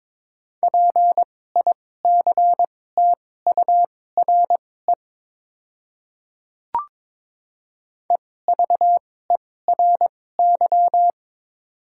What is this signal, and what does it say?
Morse code, heard as a keyed tone, about 22 wpm, spaced as for 16 wpm, 700 Hz. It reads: PICTURE EVERY
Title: 2x Character Spacing - Top 300 Words - 22wpm